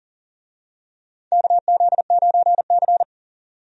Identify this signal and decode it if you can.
Morse code, heard as a keyed tone, about 40 wpm, 700 Hz. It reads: KZ9C
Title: USA Call Signs - 4 Characters - 40wpm